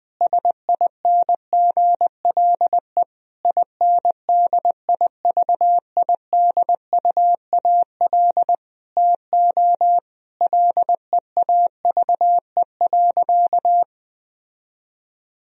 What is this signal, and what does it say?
Morse code, heard as a keyed tone, about 20 wpm, 700 Hz. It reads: SINGLE INDIVIDUAL TO LEAVE.
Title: Sentences from Top 1000 Words - Review Part 4 - 20wpm